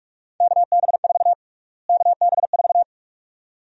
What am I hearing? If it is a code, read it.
Morse code, heard as a keyed tone, about 46 wpm, 700 Hz. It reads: KB4 KB4
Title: USA Call Sign Prefixes - Repeated 2x - 45wpm